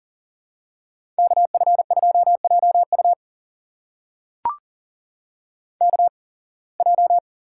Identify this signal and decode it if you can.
Morse code, heard as a keyed tone, about 39 wpm, spaced as for 26 wpm, 700 Hz. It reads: KF2JU KJ
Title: USA Call Signs - 5 Characters - 8x Character Spacing - 40wpm